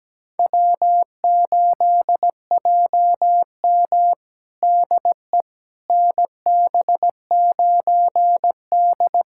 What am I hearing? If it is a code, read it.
Morse code, heard as a keyed tone, about 17 wpm, 700 Hz. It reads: W8JM DE NB9D
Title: USA Callsign DE Callsign - 4 Characters - 17wpm